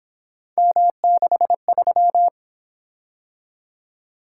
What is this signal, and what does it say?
Morse code, heard as a keyed tone, about 27 wpm, 700 Hz. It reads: M63
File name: Speed-Racing - Three Letter-Number Combinations - 22wpm